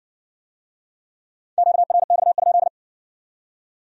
Morse code, heard as a keyed tone, about 54 wpm, 700 Hz. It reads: QRX?